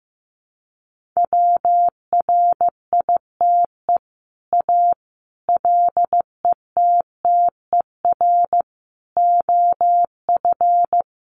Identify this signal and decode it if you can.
Morse code, heard as a keyed tone, about 14 wpm, 700 Hz. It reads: WRITE A LETTER OF
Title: Sentences from Top 1400 Words 15wpm